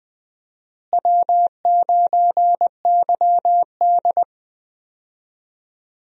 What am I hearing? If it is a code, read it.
Morse code, heard as a keyed tone, about 20 wpm, 700 Hz. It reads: W9YD
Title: USA Call Signs - 4 Characters - Repeated 3x - 20wpm